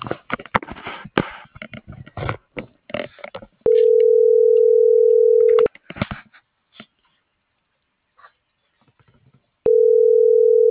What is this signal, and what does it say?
Ringback tone (North America), 440 + 480 Hz.